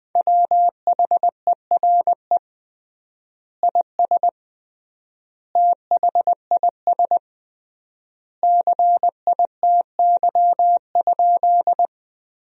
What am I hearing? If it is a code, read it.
Morse code, heard as a keyed tone, about 20 wpm, 700 Hz. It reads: WHERE IS THIS CITY?
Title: 3x Word-Spacing - Sentences from Top 300 Words - 20wpm